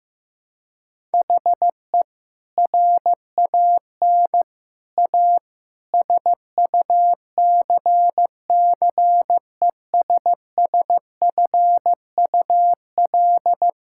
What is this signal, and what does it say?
Morse code, heard as a keyed tone, about 15 wpm, 700 Hz. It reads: HE RAN A SUCCESSFUL
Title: Sentences from Top 1000 Words - Review Part 6 - 15wpm